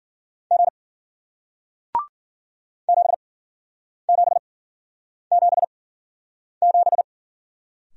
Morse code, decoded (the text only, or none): G 7777